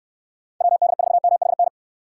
Morse code, heard as a keyed tone, about 66 wpm, 700 Hz. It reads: WD4MHN